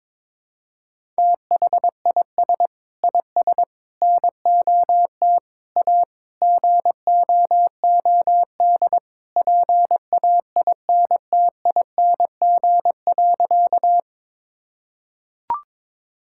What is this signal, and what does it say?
Morse code, heard as a keyed tone, about 22 wpm, 700 Hz. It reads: THIS IS NOT A GOOD PAINTING.
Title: Sentences from Top 1000 Words - Review Part 6 - 22wpm